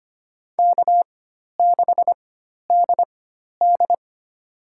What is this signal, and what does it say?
Morse code, heard as a keyed tone, about 25 wpm, spaced as for 12 wpm, 700 Hz. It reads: K6DD
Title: USA Call Signs - 4 Characters - 4x Character Spacing - 25wpm